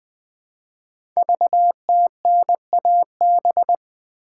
Morse code, heard as a keyed tone, about 20 wpm, 700 Hz. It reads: VTNAB
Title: Code Groups - 5 Letters - 20wpm